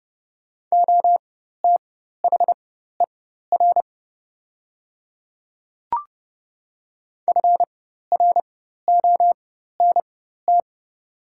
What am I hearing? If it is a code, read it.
Morse code, heard as a keyed tone, about 30 wpm, spaced as for 14 wpm, 700 Hz. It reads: OTHER FRONT
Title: 4x Character Spacing - Top 500 Words - 30wpm